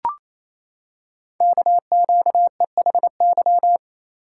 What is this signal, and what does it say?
Morse code, heard as a keyed tone, about 27 wpm, 700 Hz. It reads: KQEHY
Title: Code Groups - 5 Letters - 28wpm